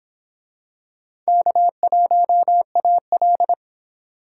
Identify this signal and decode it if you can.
Morse code, heard as a keyed tone, about 22 wpm, 700 Hz. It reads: K1AL